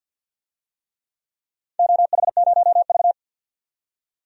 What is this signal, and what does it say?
Morse code, heard as a keyed tone, about 50 wpm, 700 Hz. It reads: OH0V
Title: DXCC-100 Call Signs - 4 Characters - 50wpm